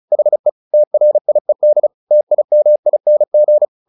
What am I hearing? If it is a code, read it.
Morse code, heard as a keyed tone, about 34 wpm, 600 Hz. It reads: HE TRIED TIMING